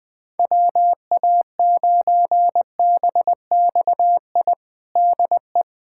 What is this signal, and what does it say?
Morse code, heard as a keyed tone, about 20 wpm, 700 Hz. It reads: WA9BXI DE
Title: Practice QSO - Exchange 4 - Ending - 20wpm